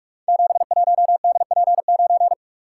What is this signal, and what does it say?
Morse code, heard as a keyed tone, about 44 wpm, 700 Hz. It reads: 71DP9
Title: Code Groups - 5 Letter:Numbers - 45wpm